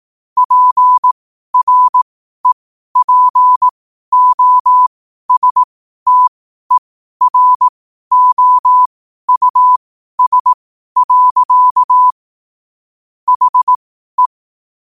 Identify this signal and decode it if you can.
Morse code, heard as a keyed tone, about 18 wpm, spaced as for 12 wpm, 1000 Hz. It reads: PREPOSTEROUS. HE
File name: qotdfiles / 12wpm